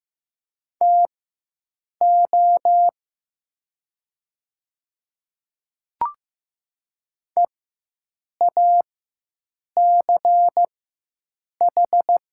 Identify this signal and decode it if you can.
Morse code, heard as a keyed tone, about 15 wpm, spaced as for 7 wpm, 700 Hz. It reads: TO EACH
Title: 4x Character Spacing - Top 100 Words - 15wpm